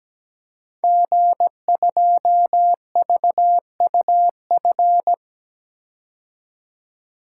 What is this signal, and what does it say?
Morse code, heard as a keyed tone, about 17 wpm, 700 Hz. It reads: G2VUF
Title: Code Groups - 5 Letter:Numbers - 17wpm